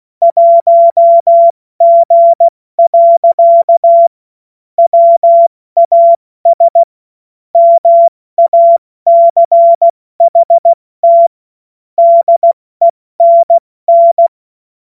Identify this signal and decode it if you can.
Morse code, heard as a keyed tone, about 16 wpm, spaced as for 14 wpm, 680 Hz. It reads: 1G. WAS MACHT DENN
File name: Morsekurs / Witze / Witz 07. Juni 2022 70 F80